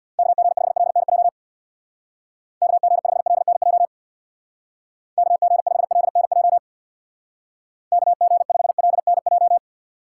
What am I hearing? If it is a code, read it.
Morse code, heard as a keyed tone, about 61 wpm, 700 Hz. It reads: KG5LNJ KG5LNJ KG5LNJ KG5LNJ
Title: Speed Racing - USA Call Signs 50wpm